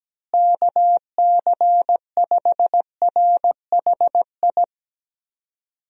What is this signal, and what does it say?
Morse code, heard as a keyed tone, about 17 wpm, 700 Hz. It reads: KC5RHI
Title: USA Call Signs - 6 Characters - Repeated 3x - 17wpm